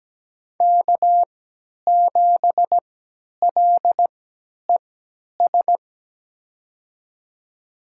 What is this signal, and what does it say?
Morse code, heard as a keyed tone, about 17 wpm, spaced as for 10 wpm, 700 Hz. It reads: K7LES
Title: USA Call Signs - 3x Character Spacing - 17wpm